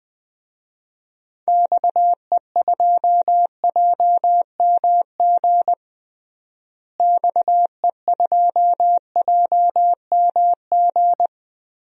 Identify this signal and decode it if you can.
Morse code, heard as a keyed tone, about 20 wpm, 700 Hz. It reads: XE2JMG XE2JMG
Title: DXCC-100 Call Signs - 6 Characters - Repeated 2x - 20wpm